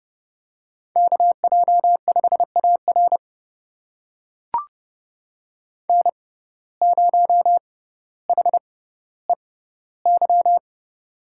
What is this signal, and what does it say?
Morse code, heard as a keyed tone, about 28 wpm, spaced as for 15 wpm, 700 Hz. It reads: KJ5AR N0HEY
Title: USA Call Signs - 5 Characters - 6x Character Spacing - 30wpm